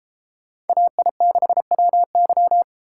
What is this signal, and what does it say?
Morse code, heard as a keyed tone, about 32 wpm, 700 Hz. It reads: AI6WY